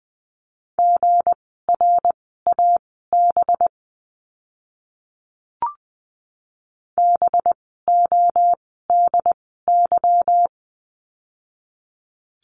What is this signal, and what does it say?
Morse code, heard as a keyed tone, about 20 wpm, spaced as for 14 wpm, 700 Hz. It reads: GRAB BODY
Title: 185 Lesson 30f - Letter b - Words 20wpm